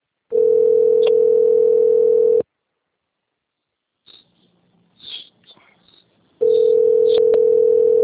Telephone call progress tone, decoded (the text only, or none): ringback tone